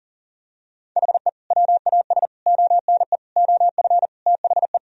Morse code, heard as a keyed tone, about 40 wpm, 700 Hz. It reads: HE WAS ONE OF THE
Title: Sentences from Top 1000 Words - Review Part 1 - 40wpm